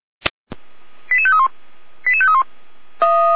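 Five-tone selective call (ZVEI), heard as quick runs of five tones, tones of about 70 ms. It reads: 88531 88531
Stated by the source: ZVEI five-tone call 89080 received on SDR.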